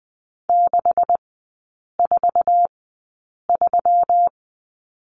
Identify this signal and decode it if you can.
Morse code, heard as a keyed tone, about 20 wpm, spaced as for 8 wpm, 700 Hz. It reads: 643
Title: 127 Lesson 21e - Number 6 - Three Characters 20wpm